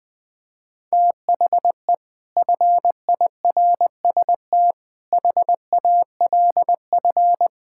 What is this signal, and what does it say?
Morse code, heard as a keyed tone, about 20 wpm, 700 Hz. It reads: THE FIRST HALF
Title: Build A Phrase - 5-Word Phrases - 20wpm